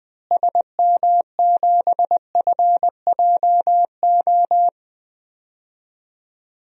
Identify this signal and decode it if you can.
Morse code, heard as a keyed tone, about 21 wpm, 700 Hz. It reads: SM7FJO
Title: DXCC-100 Call Signs - 6 Characters - Repeated 2x - 20wpm